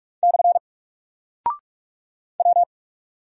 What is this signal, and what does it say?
Morse code, heard as a keyed tone, about 46 wpm, 700 Hz. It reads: / W
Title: Single Character (Letter:Number:Punctuation:Prosign) Alt 45wpm